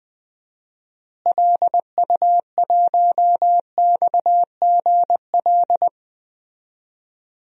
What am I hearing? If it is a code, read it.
Morse code, heard as a keyed tone, about 20 wpm, 700 Hz. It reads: LU1XGL
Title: DXCC-100 Call Signs - 6 Characters - Repeated 3x - 20wpm